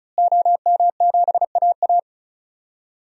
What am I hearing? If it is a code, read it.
Morse code, heard as a keyed tone, about 36 wpm, 700 Hz. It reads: OM7AA
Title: DXCC-100 Call Signs - 5 Characters - Repeated 3x - 35wpm